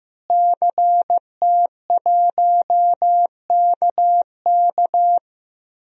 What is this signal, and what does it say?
Morse code, heard as a keyed tone, about 15 wpm, 700 Hz. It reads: CT1KK
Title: DXCC-100 Call Signs - 5 Characters - Repeated 3x - 15wpm